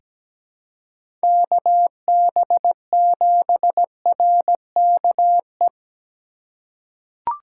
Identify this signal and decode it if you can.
Morse code, heard as a keyed tone, about 17 wpm, 700 Hz. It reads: KB7RKE
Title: USA Call Signs - 6 Characters - 3x Character Spacing - 17wpm